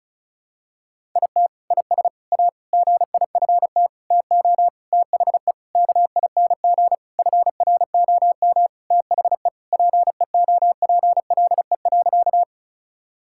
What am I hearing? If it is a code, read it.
Morse code, heard as a keyed tone, about 36 wpm, 700 Hz. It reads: IT IS A GIFT TO THE KING FROM THE PEOPLE.
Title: Sentences from Top 1000 Words - Review Part 6 - 35wpm